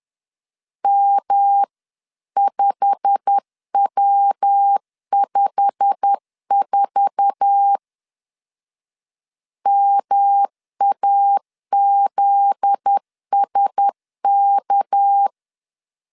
Morse code, heard as a keyed tone, about 10 wpm, 800 Hz. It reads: M5W54 MAZSK
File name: CW / DET / 1 9 1500C S B 10wpm 4439